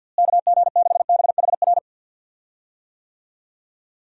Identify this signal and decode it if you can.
Morse code, heard as a keyed tone, about 52 wpm, 700 Hz. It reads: KG6BHR
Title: USA Call Signs - 6 Characters - Repeated 3x - 50wpm